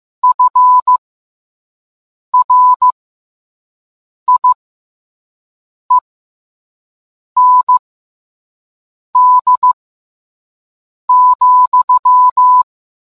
Morse code, heard as a keyed tone, about 15 wpm, spaced as for 5 wpm, 1000 Hz. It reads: FRIEND,